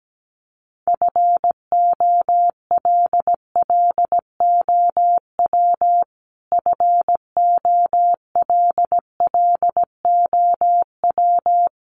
Morse code, heard as a keyed tone, about 17 wpm, 700 Hz. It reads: FOLLOW FOLLOW
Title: Top 500 Words Repeated 17wpm